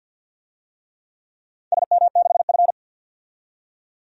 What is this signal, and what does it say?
Morse code, heard as a keyed tone, about 47 wpm, 700 Hz. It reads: SM6F
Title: DXCC-100 Call Signs - 4 Characters - 50wpm